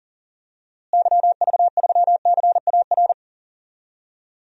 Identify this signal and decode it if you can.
Morse code, heard as a keyed tone, about 40 wpm, 700 Hz. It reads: YV3CAR
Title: DXCC-100 Call Signs - 6 Characters - Repeated 2x - 40wpm